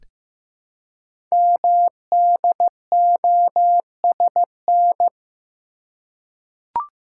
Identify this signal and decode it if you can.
Morse code, heard as a keyed tone, about 15 wpm, 700 Hz. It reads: MDOSN
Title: Code Groups - 5 Letter:Numbers - 15wpm